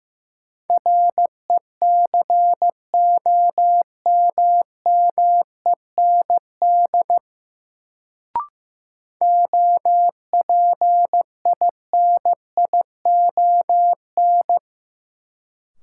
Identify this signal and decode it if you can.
Morse code, heard as a keyed tone, about 15 wpm, 700 Hz. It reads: RECOMMEND OPINION
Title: Sentences from Top 1200 Words - New Word Drill Alt - 15wpm